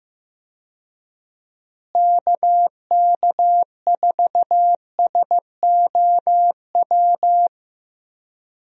Morse code, heard as a keyed tone, about 15 wpm, 700 Hz. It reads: KK4SOW
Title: USA Call Signs - 6 Characters - Repeated 2x - 15wpm